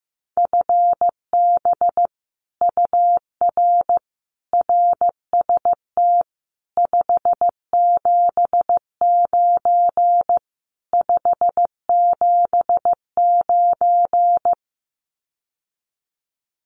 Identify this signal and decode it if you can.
Morse code, heard as a keyed tone, about 15 wpm, 700 Hz. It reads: FB UR RST 579 579
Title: Common QSO Phrases Encore 15wpm